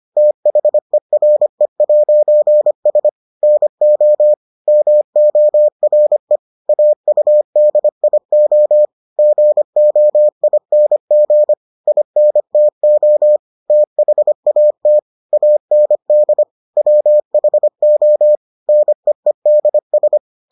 Morse code, heard as a keyed tone, about 25 wpm, 600 Hz. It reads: THERE'S NO MORE AUDIO GOING INTO THAT AND WHO NEEDS